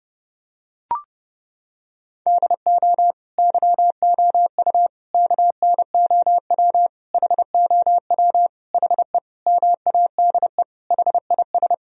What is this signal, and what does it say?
Morse code, heard as a keyed tone, about 32 wpm, 700 Hz. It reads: DO YOU KNOW HOW HE MADE HIS
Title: Sentences from Top 1000 Words - Review Part 1 - 30wpm